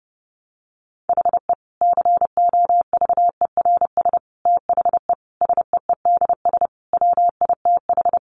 Morse code, heard as a keyed tone, about 30 wpm, 700 Hz. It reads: HE COVERS THE SEEDS WITH
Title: Easy Phrases 30wpm